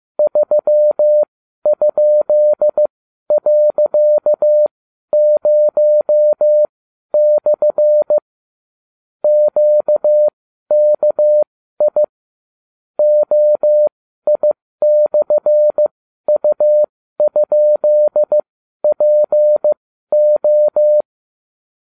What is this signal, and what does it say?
Morse code, heard as a keyed tone, about 15 wpm, spaced as for 12 wpm, 600 Hz. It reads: 3?.0/ QKI OI/U?PO